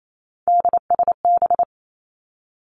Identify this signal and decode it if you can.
Morse code, heard as a keyed tone, about 28 wpm, 700 Hz. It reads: DSB
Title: Three Challenging Letter-Number Combinations 28wpm